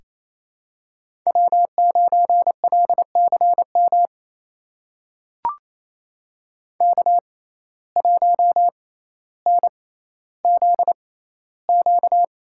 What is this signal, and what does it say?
Morse code, heard as a keyed tone, about 28 wpm, spaced as for 14 wpm, 700 Hz. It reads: W9LCM K1NZQ
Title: USA Call Signs - 5 Characters - 6x Character Spacing - 28wpm